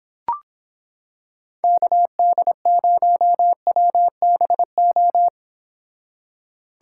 Morse code, heard as a keyed tone, about 26 wpm, 700 Hz. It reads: KD0WBO